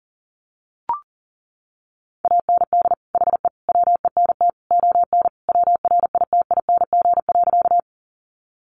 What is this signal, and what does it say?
Morse code, heard as a keyed tone, about 40 wpm, 700 Hz. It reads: AND HE WENT ON WRITING.
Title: Sentences from Top 800 Words 40wpm